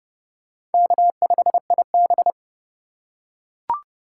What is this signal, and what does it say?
Morse code, heard as a keyed tone, about 30 wpm, 700 Hz. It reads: K5IB
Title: USA Call Signs - 4 Characters - 2x Character Spacing - 30wpm